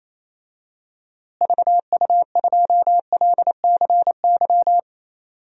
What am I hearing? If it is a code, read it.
Morse code, heard as a keyed tone, about 28 wpm, 700 Hz. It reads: VU2LCY
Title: DXCC-100 Call Signs - 6 Characters - Repeated 2x - 28wpm